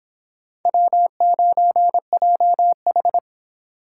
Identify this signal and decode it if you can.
Morse code, heard as a keyed tone, about 30 wpm, 700 Hz. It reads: W9JH